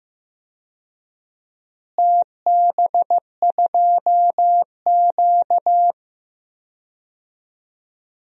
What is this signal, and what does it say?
Morse code, heard as a keyed tone, about 15 wpm, 700 Hz. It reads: TB2Q
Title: DXCC-100 Call Signs - 4 Characters - Repeated 3x - 15wpm